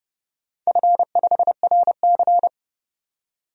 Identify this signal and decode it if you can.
Morse code, heard as a keyed tone, about 29 wpm, 700 Hz. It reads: F5RC
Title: DXCC-100 Call Signs - 4 Characters - 30wpm